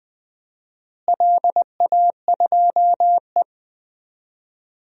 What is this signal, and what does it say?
Morse code, heard as a keyed tone, about 20 wpm, 700 Hz. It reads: LA2E
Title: DXCC-100 Call Signs - 4 Characters - Repeated 3x - 20wpm